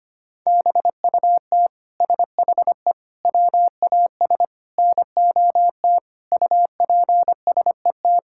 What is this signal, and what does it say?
Morse code, heard as a keyed tone, about 25 wpm, 700 Hz. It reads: BUT SHE WAS NOT UPSET
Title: Sentences from Top 1000 Words - Review Part 5 - 25wpm